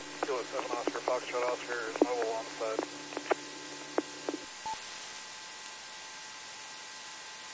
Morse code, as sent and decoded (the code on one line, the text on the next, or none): none